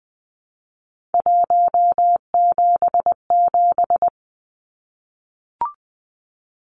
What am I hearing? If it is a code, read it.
Morse code, heard as a keyed tone, about 20 wpm, 700 Hz. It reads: 177
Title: Numbers - 3-Digit 20wpm